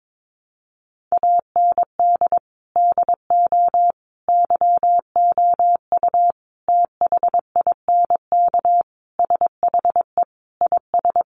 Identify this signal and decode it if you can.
Morse code, heard as a keyed tone, about 22 wpm, 700 Hz. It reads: AND DO YOU THINK SHE IS
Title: Sentences from Top 600 Words 22wpm